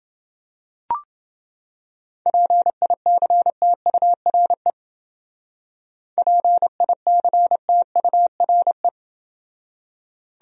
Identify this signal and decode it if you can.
Morse code, heard as a keyed tone, about 29 wpm, 700 Hz. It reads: PICTURE PICTURE